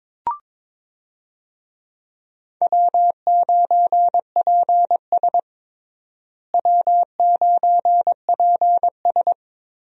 Morse code, heard as a keyed tone, about 21 wpm, 700 Hz. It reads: W9PS W9PS
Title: USA Call Signs - 4 Characters - Repeated 3x - 22wpm